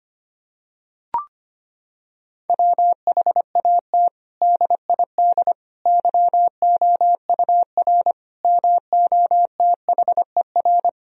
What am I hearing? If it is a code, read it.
Morse code, heard as a keyed tone, about 25 wpm, 700 Hz. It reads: WHAT DID YOUR MOTHER
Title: Sentences from Top 1000 Words - Review Part 3 - 25wpm